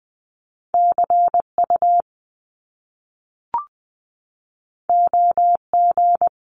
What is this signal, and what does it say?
Morse code, heard as a keyed tone, about 20 wpm, 700 Hz. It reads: CU OG
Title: Top Two Letter Combinations in Words 20wpm